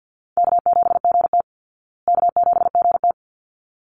Morse code, heard as a keyed tone, about 49 wpm, 700 Hz. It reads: K7GT K7GT
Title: Easy Call Signs Repeated 2x - 50wpm